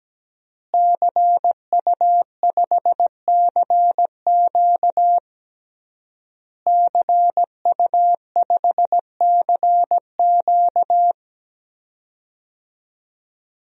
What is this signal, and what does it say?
Morse code, heard as a keyed tone, about 17 wpm, 700 Hz. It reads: CU5CQ CU5CQ
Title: DXCC-100 Call Signs - 5 Characters - Repeated 3x - 17wpm